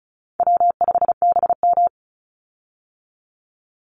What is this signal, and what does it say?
Morse code, heard as a keyed tone, about 34 wpm, 700 Hz. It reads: W5BM
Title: Easy Call Signs Repeated 2x - 35wpm